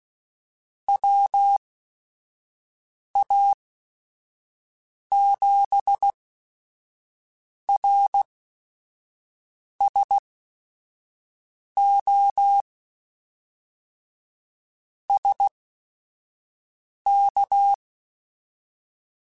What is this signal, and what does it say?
Morse code, heard as a keyed tone, about 16 wpm, spaced as for 5 wpm, 800 Hz. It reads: WA7RSO SK